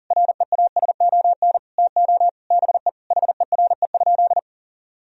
Morse code, heard as a keyed tone, about 40 wpm, 700 Hz. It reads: REASON TO BE HERE?
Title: Sentences from Top 1000 Words - Review Part 4 - 40wpm